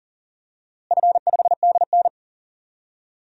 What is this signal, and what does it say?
Morse code, heard as a keyed tone, about 40 wpm, 700 Hz. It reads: F5DN
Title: DXCC-100 Call Signs - 4 Characters - Repeated 2x - 40wpm